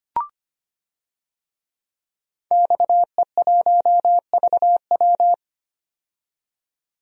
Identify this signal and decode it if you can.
Morse code, heard as a keyed tone, about 25 wpm, 700 Hz. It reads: XE1VW
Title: DXCC-100 Call Signs - 5 Characters - 25wpm